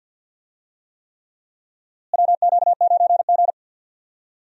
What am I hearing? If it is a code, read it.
Morse code, heard as a keyed tone, about 52 wpm, 700 Hz. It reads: WQ9G
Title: USA Call Signs - 4 Characters - Repeated 3x - 50wpm